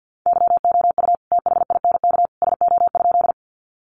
Morse code, heard as a keyed tone, about 50 wpm, 700 Hz. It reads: YOU THINK SO?
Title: Sentences from Top 200 Words 50wpm